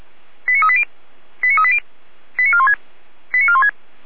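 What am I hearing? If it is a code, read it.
Five-tone selective call (ZVEI), heard as quick runs of five tones, tones of about 70 ms. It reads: 89380 89380 89416 89416
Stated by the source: ZVEI five-tone selective call 89582, received with an SDR, archived by Feuerwehr Orlamünde.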